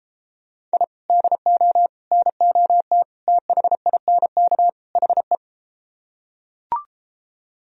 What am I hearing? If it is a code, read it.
Morse code, heard as a keyed tone, about 32 wpm, 700 Hz. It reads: I DO NOT THINK HE